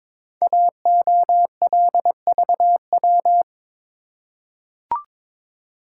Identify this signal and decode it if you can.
Morse code, heard as a keyed tone, about 22 wpm, 700 Hz. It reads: AOLVW
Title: Code Groups - 5 Letters - 22wpm